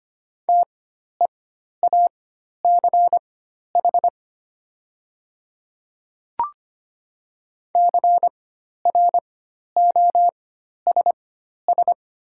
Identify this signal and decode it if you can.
Morse code, heard as a keyed tone, about 25 wpm, spaced as for 12 wpm, 690 Hz. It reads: TEACH CROSS
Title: 4x Character Spacing - Top 500 Words - 25wpm